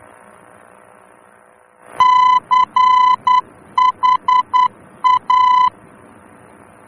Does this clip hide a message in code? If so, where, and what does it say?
Yes: Morse code from 2.0 to 5.7 s, heard as a keyed tone, about 10 wpm, 1010 Hz. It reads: CHA